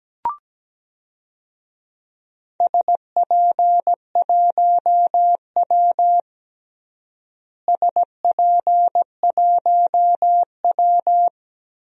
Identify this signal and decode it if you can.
Morse code, heard as a keyed tone, about 17 wpm, 700 Hz. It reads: SP1W SP1W
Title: DXCC-100 Call Signs - 4 Characters - Repeated 3x - 17wpm